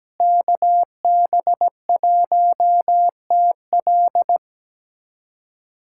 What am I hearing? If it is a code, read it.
Morse code, heard as a keyed tone, about 17 wpm, 700 Hz. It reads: KB1TL
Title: USA Callsign DE Callsign - 5 Characters - 17wpm